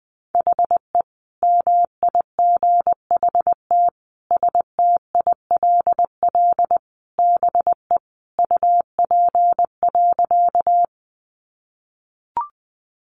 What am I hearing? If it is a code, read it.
Morse code, heard as a keyed tone, about 20 wpm, 700 Hz. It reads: HE MIGHT STILL BE UP.
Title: Sentences from Top 300 Words 20wpm